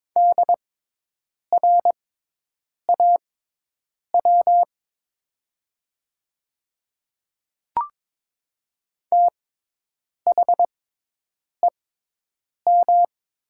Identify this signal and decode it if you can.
Morse code, heard as a keyed tone, about 22 wpm, spaced as for 8 wpm, 700 Hz. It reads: DRAW THEM
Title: 6x Character Spacing - Top 300 Words - 22wpm